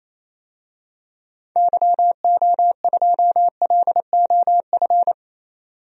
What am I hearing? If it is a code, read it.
Morse code, heard as a keyed tone, about 28 wpm, 700 Hz. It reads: YO2LOF
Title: DXCC-100 Call Signs - 6 Characters - 28wpm